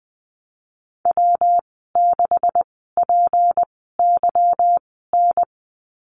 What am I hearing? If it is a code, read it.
Morse code, heard as a keyed tone, about 20 wpm, spaced as for 15 wpm, 700 Hz. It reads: W6PYN
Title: 186 Lesson 30g - Letter b - Call Signs 20wpm